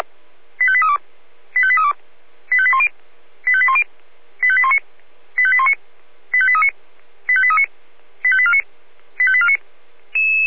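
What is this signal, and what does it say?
Five-tone selective call (ZVEI), heard as quick runs of five tones, tones of about 70 ms. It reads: 85832 85832 85810 85810 85819 85819 85839 85839 85849 85849
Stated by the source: Five-tone ZVEI selective call received on an SDR.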